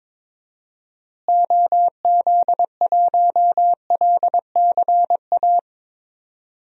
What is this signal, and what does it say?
Morse code, heard as a keyed tone, about 22 wpm, 700 Hz. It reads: OZ1LCA
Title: DXCC-100 Call Signs - 6 Characters - Repeated 2x - 22wpm